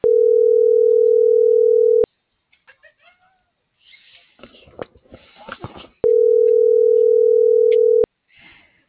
Ringback tone (North America), 440 + 480 Hz.